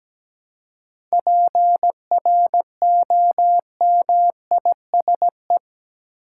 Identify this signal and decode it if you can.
Morse code, heard as a keyed tone, about 17 wpm, 700 Hz. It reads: PROMISE
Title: Seven Letter Words - 17wpm